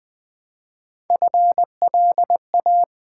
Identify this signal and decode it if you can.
Morse code, heard as a keyed tone, about 20 wpm, 690 Hz. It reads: FLA
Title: Build A Word - 1 Letter At A Time - 5-Letter Words - 20wpm